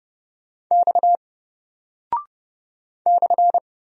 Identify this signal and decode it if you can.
Morse code, heard as a keyed tone, about 31 wpm, 700 Hz. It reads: X /
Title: Single Character (Letter:Number:Punctuation:Prosign) 30wpm